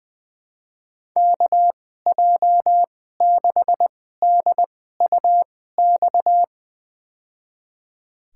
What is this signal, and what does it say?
Morse code, heard as a keyed tone, about 20 wpm, spaced as for 14 wpm, 700 Hz. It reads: KJ6DUX
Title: USA Call Signs - 6 Characters - 2x Character Spacing - 20wpm